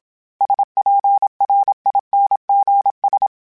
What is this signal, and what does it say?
Morse code, heard as a keyed tone, about 27 wpm, 800 Hz. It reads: SPRINGS
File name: fromSMs / FromSMs mp3s / Cities@25